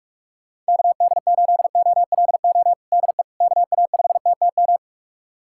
Morse code, heard as a keyed tone, about 41 wpm, 700 Hz. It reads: KD8OLO DE KA5TTM
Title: USA Callsign DE Callsign 45wpm